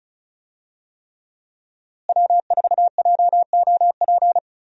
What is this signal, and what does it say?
Morse code, heard as a keyed tone, about 36 wpm, 700 Hz. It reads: W4JOP
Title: USA Call Signs - 5 Characters - Repeated 3x - 35wpm